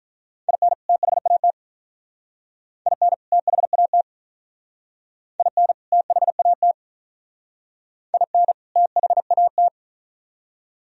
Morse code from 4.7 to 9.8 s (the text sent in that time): IN THAT IN THAT